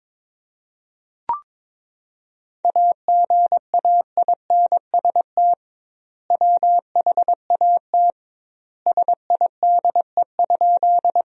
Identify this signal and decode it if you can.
Morse code, heard as a keyed tone, about 21 wpm, 700 Hz. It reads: AGAINST WHAT SIDE?
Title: 2x Word-Spacing - Sentences from Top 400 Words - 22wpm